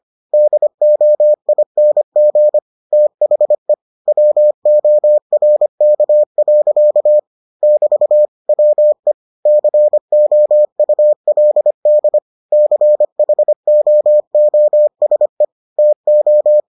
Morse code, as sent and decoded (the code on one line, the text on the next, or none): -.. --- .. -. --. / - .... / .-- --- .-. -.- .-.-.- / -...- / .-- . / -.-. --- ..- .-.. -.. / -.-. .... --- --- ... / - ---
DOING THE WORK. <BT> WE COULD CHOOSE TO